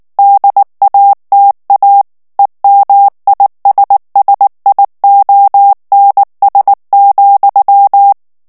Morse code, heard as a keyed tone, about 19 wpm, 800 Hz. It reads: DATA EMISSIONS,
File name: morse mp3 / ANGLAIS texte clair 3 vitesse 18 mots minute